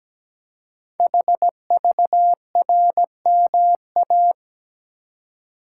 Morse code, heard as a keyed tone, about 17 wpm, 700 Hz. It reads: HVRMA
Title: Code Groups - 5 Letters - 17wpm